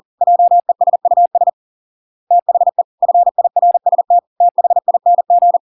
Morse code, heard as a keyed tone, about 40 wpm, 700 Hz. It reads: JESUS THE FIRST THING